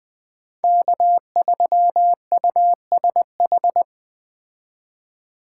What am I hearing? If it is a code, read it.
Morse code, heard as a keyed tone, about 20 wpm, 700 Hz. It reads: K3USH
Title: Speed Racing - USA 1x3 Call Signs 17wpm